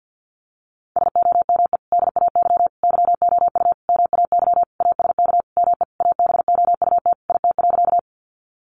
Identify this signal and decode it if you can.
Morse code, heard as a keyed tone, about 50 wpm, 700 Hz. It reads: SOME DAY YOU MAY ASK ME ABOUT IT.
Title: Sentences from Top 200 Words 50wpm